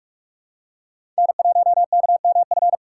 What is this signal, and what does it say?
Morse code, heard as a keyed tone, about 46 wpm, 700 Hz. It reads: N1KMF
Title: USA Call Signs - 3x Character Spacing - 45wpm